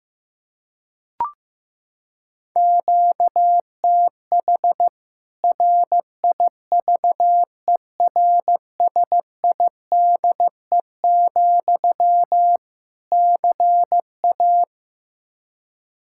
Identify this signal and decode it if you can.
Morse code, heard as a keyed tone, about 15 wpm, 700 Hz. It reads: QTH RIVERSIDE, CA
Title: Practice QSO - Exchange 1 - Beginning - 15wpm